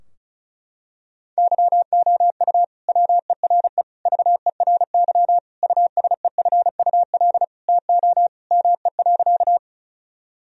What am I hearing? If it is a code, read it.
Morse code, heard as a keyed tone, about 34 wpm, 700 Hz. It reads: YOU WERE VERY USEFUL TO ME.
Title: Sentences from Top 1000 Words - Review Part 4 - 35wpm